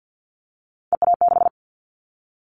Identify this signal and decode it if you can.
Morse code, heard as a keyed tone, about 50 wpm, 700 Hz. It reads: EA6
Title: Callsign Prefixes for Top 100 DXCC - 50wpm